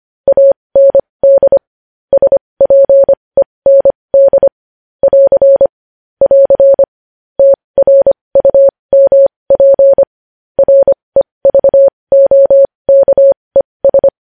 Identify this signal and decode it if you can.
Morse code, heard as a keyed tone, about 25 wpm, spaced as for 20 wpm, 550 Hz. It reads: AND SPEND <AR> <AR> TRUMP REVOKES
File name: podcast / News headlines-1x-25x20